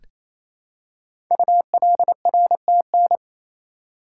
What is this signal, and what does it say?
Morse code, heard as a keyed tone, about 27 wpm, 700 Hz. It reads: ULRTN
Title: Code Groups - 5 Letters - 28wpm